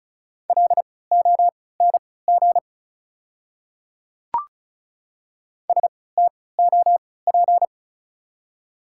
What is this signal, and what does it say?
Morse code, heard as a keyed tone, about 34 wpm, spaced as for 20 wpm, 700 Hz. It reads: LONG STOP
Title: 3x Character Spacing - Top 300 Words - 35wpm